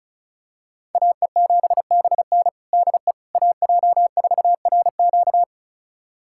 Morse code, heard as a keyed tone, about 34 wpm, 700 Hz. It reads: AE7BN DE AJ4RQ
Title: USA Callsign DE Callsign - 5 Characters - 35wpm